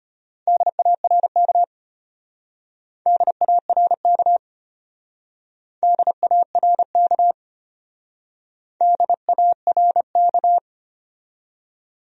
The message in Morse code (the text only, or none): DARK DARK DARK DARK